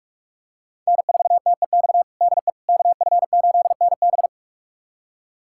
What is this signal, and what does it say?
Morse code, heard as a keyed tone, about 46 wpm, 700 Hz. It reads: N4TEX DE KF8NB
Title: USA Callsign DE Callsign - 5 Characters - 45wpm